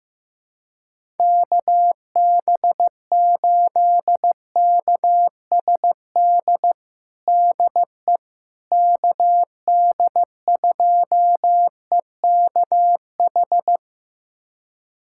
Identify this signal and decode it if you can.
Morse code, heard as a keyed tone, about 15 wpm, 700 Hz. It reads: KB8KSD DE KD2EKH
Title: USA Callsign DE Callsign 15wpm